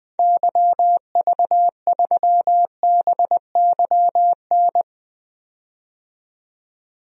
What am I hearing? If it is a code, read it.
Morse code, heard as a keyed tone, about 21 wpm, 700 Hz. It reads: YV3BYN
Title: DXCC-100 Call Signs - 6 Characters - 20wpm